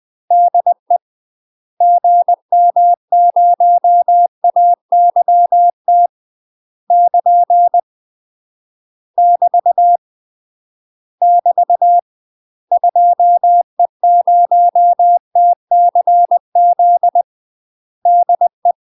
Morse code, heard as a keyed tone, about 20 wpm, 700 Hz. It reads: DE GM0AYT <KN> <BT> <BT> 2E0TCZ DE
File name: UK MASTER QSOs TRAINING / 20WPM / UK MASTER QSOs 0003